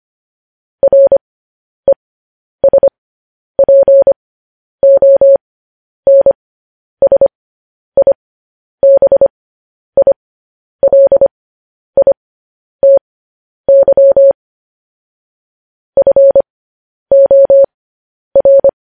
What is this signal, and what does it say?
Morse code, heard as a keyed tone, about 25 wpm, spaced as for 10 wpm, 550 Hz. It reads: RESPONSIBILITY FOR
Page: news-headlines-in-morse-code-at-10-wpm